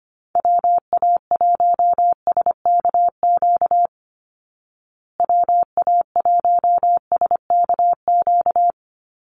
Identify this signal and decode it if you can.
Morse code, heard as a keyed tone, about 25 wpm, 700 Hz. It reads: WA1SKQ WA1SKQ
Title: Call Signs Repeated 2x - 25wpm